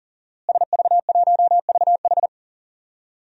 Morse code, heard as a keyed tone, about 40 wpm, 700 Hz. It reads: SV1VH
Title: DXCC-100 Call Signs - 5 Characters - Repeated 2x - 40wpm